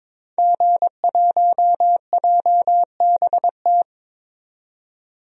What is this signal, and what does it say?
Morse code, heard as a keyed tone, about 22 wpm, 700 Hz. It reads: G1JBT
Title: DXCC-100 Call Signs - 5 Characters - 22wpm